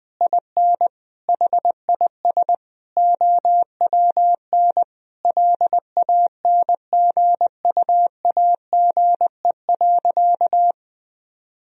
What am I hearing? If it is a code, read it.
Morse code, heard as a keyed tone, about 20 wpm, 700 Hz. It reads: IN HIS OWN LANGUAGE.
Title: Sentences from Top 1000 Words - Review Part 3 - 20wpm